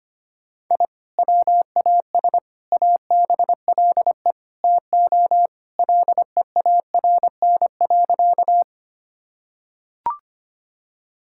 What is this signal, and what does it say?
Morse code, heard as a keyed tone, about 25 wpm, 700 Hz. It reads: I WAS ABLE TO LEARN.
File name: practice / 25wpm / Sentences with 4-Word Phrases - 25wpm